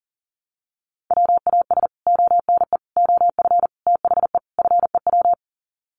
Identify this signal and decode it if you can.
Morse code, heard as a keyed tone, about 40 wpm, 700 Hz. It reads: WAS ONE OF THE FEW
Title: Sets of 5 Words Encore 40wpm